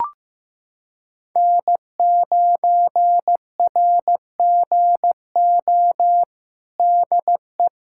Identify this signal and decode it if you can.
Morse code, heard as a keyed tone, about 15 wpm, 700 Hz. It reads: N9RGO DE
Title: USA Callsign DE Callsign - 5 Characters - 15wpm